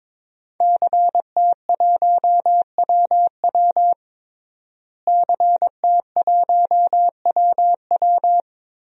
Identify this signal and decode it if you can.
Morse code, heard as a keyed tone, about 22 wpm, 700 Hz. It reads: CT1WW CT1WW
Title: DXCC-100 Call Signs - 5 Characters - Repeated 3x - 22wpm